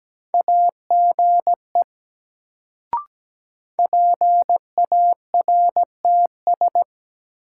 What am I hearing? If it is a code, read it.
Morse code, heard as a keyed tone, about 17 wpm, 700 Hz. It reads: AGE PARTS
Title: Sets of 3 Words - New Word Drill - 17wpm